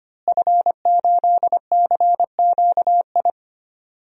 Morse code, heard as a keyed tone, about 25 wpm, 700 Hz. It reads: F8CQI
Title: DXCC-100 Call Signs - 5 Characters - Repeated 3x - 25wpm